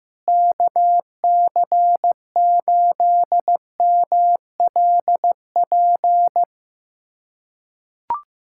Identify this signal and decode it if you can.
Morse code, heard as a keyed tone, about 15 wpm, 700 Hz. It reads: KC8MLP
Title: USA Callsign DE Callsign 15wpm